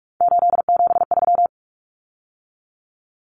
Morse code, heard as a keyed tone, about 44 wpm, 700 Hz. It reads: 873
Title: Common Cut Numbers - 3-Digit 45wpm